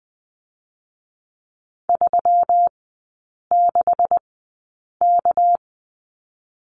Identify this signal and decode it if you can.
Morse code, heard as a keyed tone, about 20 wpm, spaced as for 8 wpm, 700 Hz. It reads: 36K
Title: 198 Lesson 32e - Letter k - Three Characters 20wpm